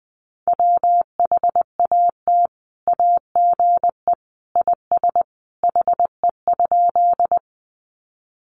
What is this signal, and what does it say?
Morse code, heard as a keyed tone, about 20 wpm, 700 Hz. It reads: WHAT AGE IS HE?
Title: Sentences from Top 500 Words 20wpm